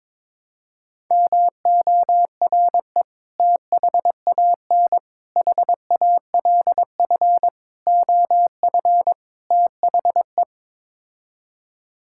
Morse code, heard as a keyed tone, about 22 wpm, 700 Hz. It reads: MORE THAN HALF OF THE
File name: practice / 22wpm / Build A Phrase - 5-Word Phrases - 22wpm